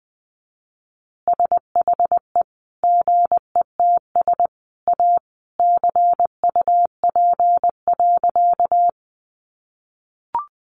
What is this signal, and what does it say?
Morse code, heard as a keyed tone, about 20 wpm, 700 Hz. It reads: SHE GETS A CUP.
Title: Additional Easy Phrases 20wpm